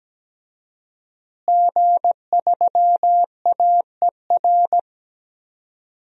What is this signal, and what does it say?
Morse code, heard as a keyed tone, about 17 wpm, 700 Hz. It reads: G3AER
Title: DXCC-100 Call Signs - 5 Characters - Repeated 3x - 17wpm